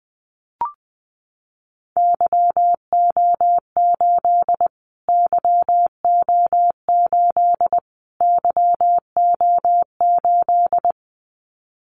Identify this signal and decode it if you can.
Morse code, heard as a keyed tone, about 20 wpm, 700 Hz. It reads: YO8 YO8 YO8
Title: Callsign Prefixes for Top 25 DXCC - Repeated 3x - 20wpm